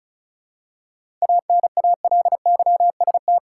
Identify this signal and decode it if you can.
Morse code, heard as a keyed tone, about 36 wpm, 700 Hz. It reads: ANALYST
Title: Seven Letter Words - 35wpm